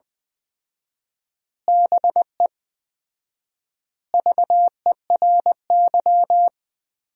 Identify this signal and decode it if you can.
Morse code, heard as a keyed tone, about 20 wpm, 700 Hz. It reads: BE VERY
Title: 4x Word-Spacing - Sentences from Top 200 Words - 20wpm